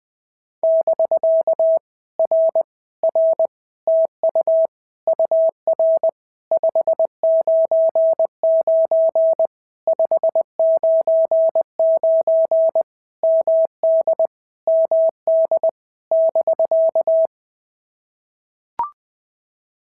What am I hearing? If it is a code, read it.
Morse code, heard as a keyed tone, about 20 wpm, 650 Hz. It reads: <BK> R R TU UR 599 599 MD MD <BK>
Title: POTA QSO Practice - 20wpm